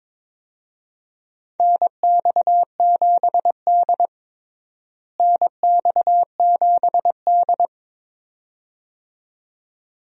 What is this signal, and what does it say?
Morse code, heard as a keyed tone, about 21 wpm, 700 Hz. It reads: NX7D NX7D
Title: USA Call Signs - 4 Characters - Repeated 2x - 22wpm